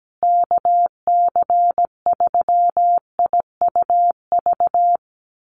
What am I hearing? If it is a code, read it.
Morse code, heard as a keyed tone, about 17 wpm, 700 Hz. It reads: KC3IUV
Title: USA Call Signs Repeated 2x 17wpm